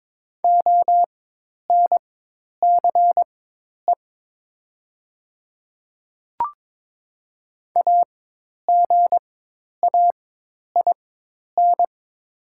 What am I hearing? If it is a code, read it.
Morse code, heard as a keyed tone, about 22 wpm, spaced as for 10 wpm, 700 Hz. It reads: ONCE AGAIN
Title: 4x Character Spacing - Top 300 Words - 22wpm